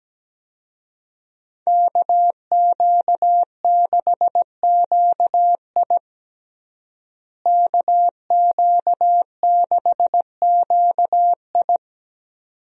Morse code, heard as a keyed tone, about 17 wpm, 700 Hz. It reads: KQ6QI KQ6QI
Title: USA Call Signs - 5 Characters - Repeated 2x - 17wpm